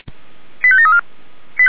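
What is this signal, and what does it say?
Five-tone selective call (ZVEI), heard as quick runs of five tones, tones of about 70 ms. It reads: 85734
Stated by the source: ZVEI five-tone selective call received by SDR.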